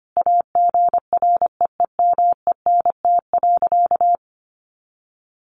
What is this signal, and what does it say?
Morse code, heard as a keyed tone, about 25 wpm, 700 Hz. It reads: AGREEMENT.
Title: Sentences from Top 1500 Words 25wpm